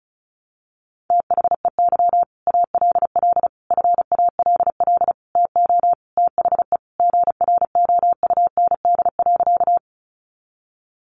Morse code, heard as a keyed tone, about 34 wpm, 700 Hz. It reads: THEY ALL FALL TO THE GROUND.
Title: Sentences from Top 500 Words 35wpm